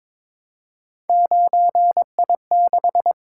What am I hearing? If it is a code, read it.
Morse code, heard as a keyed tone, about 22 wpm, 700 Hz. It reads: 9I6
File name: practice / 22wpm / Three Letter-Number Combinations - 22wpm